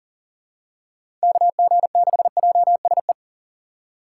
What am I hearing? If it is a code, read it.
Morse code, heard as a keyed tone, about 40 wpm, 700 Hz. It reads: KG6JSE
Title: USA Call Signs - 6 Characters - 4x Character Spacing - 40wpm